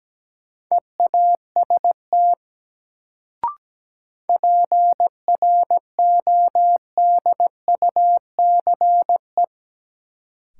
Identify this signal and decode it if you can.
Morse code, heard as a keyed tone, about 17 wpm, 700 Hz. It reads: EAST PRODUCE
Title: Sentences from Top 1000 Words - Review Part 3 - New Word Drill - 17wpm